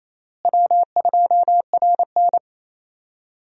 Morse code, heard as a keyed tone, about 27 wpm, 700 Hz. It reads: W2RN
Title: USA Call Signs - 4 Characters - Repeated 3x - 28wpm